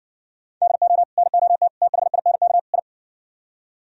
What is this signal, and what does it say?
Morse code, heard as a keyed tone, about 60 wpm, 700 Hz. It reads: DO NOT THINK I